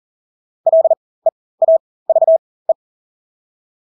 Morse code, heard as a keyed tone, about 40 wpm, spaced as for 20 wpm, 650 Hz. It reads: LEAVE
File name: Engelske ord 40-20 wpm